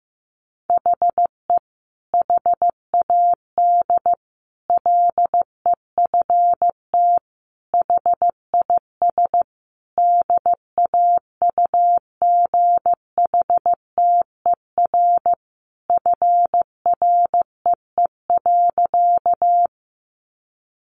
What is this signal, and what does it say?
Morse code, heard as a keyed tone, about 15 wpm, 700 Hz. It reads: HE HAD LEFT HIS DAUGHTER FREE.
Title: Sentences from Top 1500 Words 15wpm